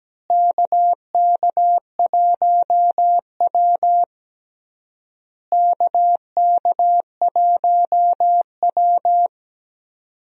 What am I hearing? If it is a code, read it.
Morse code, heard as a keyed tone, about 17 wpm, 700 Hz. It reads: KK1W KK1W
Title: USA Call Signs - 4 Characters - Repeated 3x - 17wpm